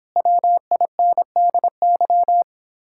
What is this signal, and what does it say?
Morse code, heard as a keyed tone, about 31 wpm, 700 Hz. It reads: WINDY